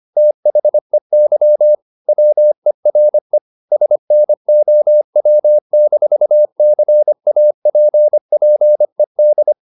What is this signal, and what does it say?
Morse code, heard as a keyed tone, about 25 wpm, 600 Hz. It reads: THEY WERE SNOW-CAPPED
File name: the old man and the sea part 1 25wpm